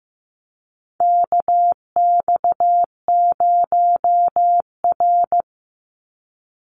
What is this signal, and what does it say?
Morse code, heard as a keyed tone, about 15 wpm, 700 Hz. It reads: KX0R
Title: NA SOTA Activator Callsigns 15wpm